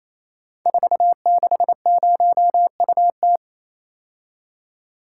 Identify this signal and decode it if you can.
Morse code, heard as a keyed tone, about 28 wpm, 700 Hz. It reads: 460UT
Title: Code Groups - 5 Letter:Numbers - 28wpm